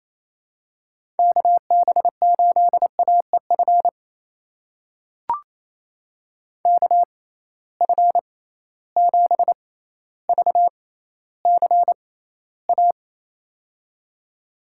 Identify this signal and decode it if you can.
Morse code, heard as a keyed tone, about 28 wpm, spaced as for 14 wpm, 700 Hz. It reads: KB8AEF KF7VCA
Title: USA Call Signs - 6 Characters - 6x Character Spacing - 28wpm